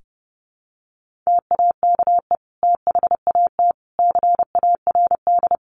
Morse code, heard as a keyed tone, about 28 wpm, 700 Hz. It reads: TAKE THAT CARD
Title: Sentences from Top 1400 Words 30wpm